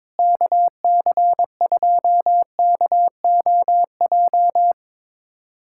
Morse code, heard as a keyed tone, about 22 wpm, 700 Hz. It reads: KC2KOJ
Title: USA Callsign DE Callsign 22wpm